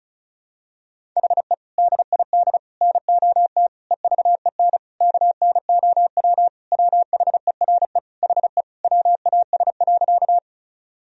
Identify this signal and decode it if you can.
Morse code, heard as a keyed tone, about 36 wpm, 700 Hz. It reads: HE DID NOT EVEN KNOW WHERE HE WAS.
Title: Sentences from Top 1000 Words - Review Part 1 - 35wpm